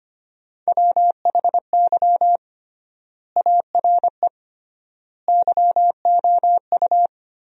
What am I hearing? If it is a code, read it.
Morse code, heard as a keyed tone, about 25 wpm, 700 Hz. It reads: WHY ARE YOU
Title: 3x Word-Spacing - Sentences from Top 200 Words - 25wpm